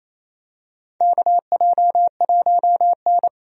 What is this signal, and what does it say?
Morse code, heard as a keyed tone, about 28 wpm, 700 Hz. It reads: KJ1N
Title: USA Call Signs - 4 Characters - Repeated 3x - 28wpm